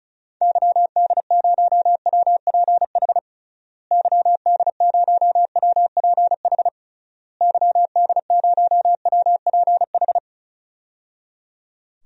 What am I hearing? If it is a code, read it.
Morse code, heard as a keyed tone, about 36 wpm, 700 Hz. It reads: YD0WPH YD0WPH YD0WPH
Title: DXCC-100 Call Signs - 6 Characters - Repeated 3x - 35wpm